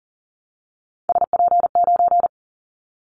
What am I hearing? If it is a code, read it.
Morse code, heard as a keyed tone, about 40 wpm, 700 Hz. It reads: SP9
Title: Callsign Prefixes for Top 25 DXCC - Repeated 3x - 40wpm